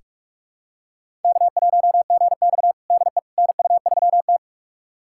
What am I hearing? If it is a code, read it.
Morse code, heard as a keyed tone, about 44 wpm, 690 Hz. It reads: K1GX DE NU3T
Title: USA Callsign DE Callsign - 4 Characters - 45wpm